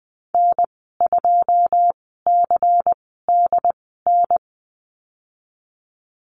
Morse code, heard as a keyed tone, about 19 wpm, spaced as for 14 wpm, 700 Hz. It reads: N2CDN
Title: 167 Lesson 27g - Number 7 - Call Signs 20wpm